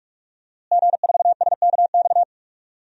Morse code, heard as a keyed tone, about 44 wpm, 700 Hz. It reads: G4SKX
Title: DXCC-100 Call Signs - 5 Characters - Repeated 2x - 45wpm